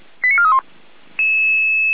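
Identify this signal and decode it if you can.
Five-tone selective call (ZVEI), heard as quick runs of five tones, tones of about 70 ms. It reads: 89431
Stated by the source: ZVEI five-tone selective call 88064 received by SDR.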